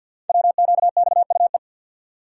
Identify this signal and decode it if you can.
Morse code, heard as a keyed tone, about 50 wpm, 700 Hz. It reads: WQXUE
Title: Code Groups - 5 Letters - 50wpm